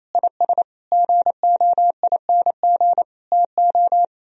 Morse code, heard as a keyed tone, about 35 wpm, 700 Hz. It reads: IS GOING TO